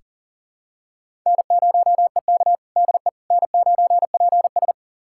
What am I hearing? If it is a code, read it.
Morse code, heard as a keyed tone, about 37 wpm, 700 Hz. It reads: N0EK DE N9PS
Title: USA Callsign DE Callsign - 4 Characters - 40wpm